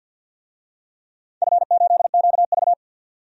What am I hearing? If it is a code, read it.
Morse code, heard as a keyed tone, about 53 wpm, 700 Hz. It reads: F8QV
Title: DXCC-100 Call Signs - 4 Characters - 50wpm